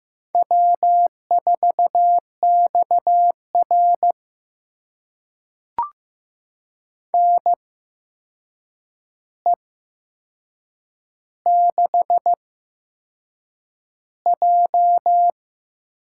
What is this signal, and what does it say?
Morse code, heard as a keyed tone, about 15 wpm, spaced as for 6 wpm, 700 Hz. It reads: W4XR NE6J
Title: USA Call Signs - 4 Characters - 8x Character Spacing - 15wpm